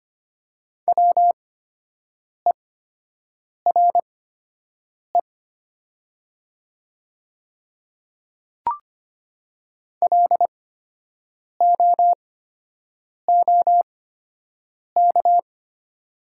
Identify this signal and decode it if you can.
Morse code, heard as a keyed tone, about 25 wpm, spaced as for 7 wpm, 700 Hz. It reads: WERE LOOK
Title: 8x Character Spacing - Top 100 Words - 25wpm